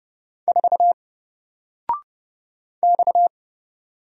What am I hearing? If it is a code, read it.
Morse code, heard as a keyed tone, about 30 wpm, 700 Hz. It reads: 4 X
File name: Single-Letter-Number-30wpm